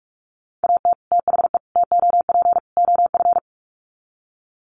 Morse code, heard as a keyed tone, about 44 wpm, 700 Hz. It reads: AT THE TOP OF
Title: Sets of 4 Words 45wpm